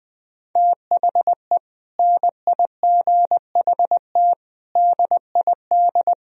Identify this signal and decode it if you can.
Morse code, heard as a keyed tone, about 20 wpm, 700 Hz. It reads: THE NIGHT DID
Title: Sentences from Top 1000 Words - Review Part 1 - 20wpm